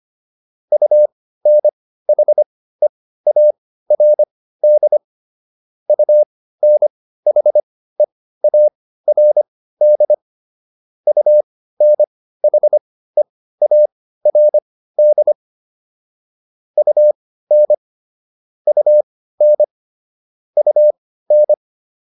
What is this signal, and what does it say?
Morse code, heard as a keyed tone, about 25 wpm, spaced as for 15 wpm, 600 Hz. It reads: UNHEARD UNHEARD UNHEARD UN UN UN